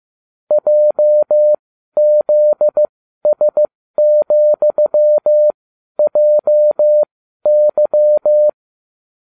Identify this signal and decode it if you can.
Morse code, heard as a keyed tone, about 15 wpm, spaced as for 12 wpm, 600 Hz. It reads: JZS,JY